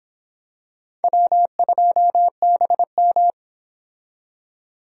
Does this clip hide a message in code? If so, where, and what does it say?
Yes: Morse code from 1.0 to 3.3 s, heard as a keyed tone, about 25 wpm, 700 Hz. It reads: W2BM